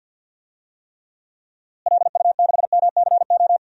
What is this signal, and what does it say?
Morse code, heard as a keyed tone, about 53 wpm, 700 Hz. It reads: LU6MCO
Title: DXCC-100 Call Signs - 6 Characters - Repeated 2x - 50wpm